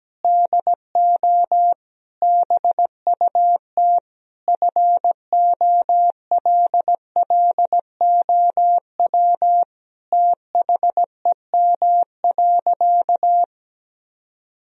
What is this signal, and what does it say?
Morse code, heard as a keyed tone, about 17 wpm, 700 Hz. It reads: DO BUT FOLLOW THEM.
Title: Sentences from Top 1000 Words - Review Part 1 - 17wpm